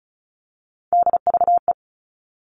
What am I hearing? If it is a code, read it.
Morse code, heard as a keyed tone, about 34 wpm, 700 Hz. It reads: DVE
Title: All Cut Numbers - 3-Digit 35wpm